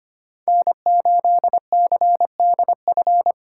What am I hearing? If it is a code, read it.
Morse code, heard as a keyed tone, about 33 wpm, 700 Hz. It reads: N8CDF